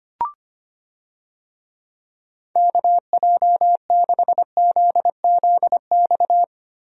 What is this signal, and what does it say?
Morse code, heard as a keyed tone, about 25 wpm, 700 Hz. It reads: KJ6ZZX
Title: USA Call Signs - 6 Characters - Repeated 2x - 25wpm